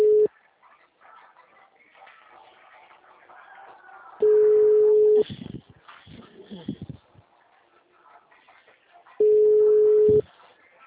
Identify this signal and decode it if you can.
Ringback tone (Europe, 425 Hz), with speech.